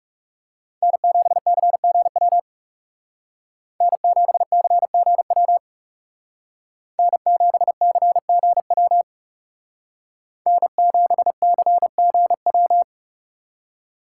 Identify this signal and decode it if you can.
Morse code, heard as a keyed tone, about 38 wpm, 700 Hz. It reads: N7CGW N7CGW N7CGW N7CGW